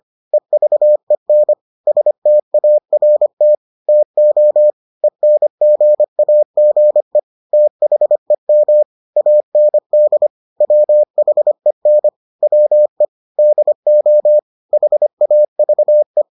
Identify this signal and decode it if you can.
Morse code, heard as a keyed tone, about 25 wpm, 600 Hz. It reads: EVEN START TO ENGAGE THEM AND WHEN WE DO HAVE